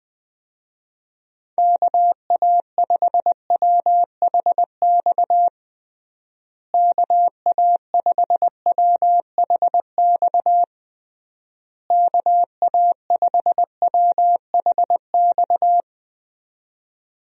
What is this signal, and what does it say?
Morse code, heard as a keyed tone, about 21 wpm, 700 Hz. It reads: KA5WHX KA5WHX KA5WHX
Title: USA Call Signs - 6 Characters - Repeated 3x - 20wpm